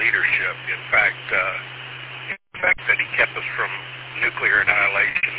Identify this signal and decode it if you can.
Emergency Alert System (SAME) tones, heard as harsh buzzing bursts, with speech. It is heard from 0.8 to 4.8 s.